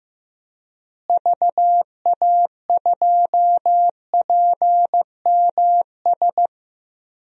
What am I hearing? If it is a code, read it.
Morse code, heard as a keyed tone, about 15 wpm, 700 Hz. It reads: VA2PMS
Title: DXCC-100 Call Signs - 6 Characters - Repeated 2x - 15wpm